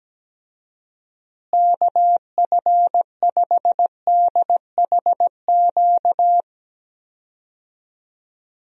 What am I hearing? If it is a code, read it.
Morse code, heard as a keyed tone, about 17 wpm, 700 Hz. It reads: KF5DHQ
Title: USA Call Signs - 6 Characters - Repeated 3x - 17wpm